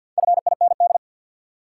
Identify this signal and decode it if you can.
Morse code, heard as a keyed tone, about 53 wpm, 700 Hz. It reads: FIND
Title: Sentences from Top 100 Words - New Word Drill - 50wpm